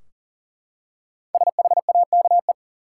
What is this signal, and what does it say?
Morse code, heard as a keyed tone, about 40 wpm, 700 Hz. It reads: SHAKE
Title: Five Letter Words - 40wpm